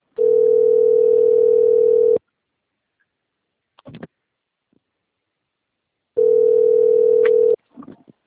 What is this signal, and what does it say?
Ringback tone (North America), 440 + 480 Hz.